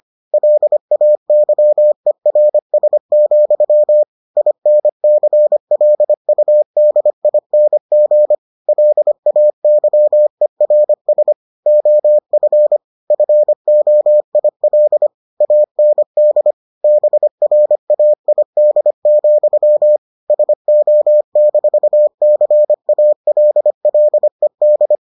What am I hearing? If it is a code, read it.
Morse code, heard as a keyed tone, about 25 wpm, 600 Hz. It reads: LAYERS, INCLUDING LAYERS OF FOIL AND BRAID, SO-CALLED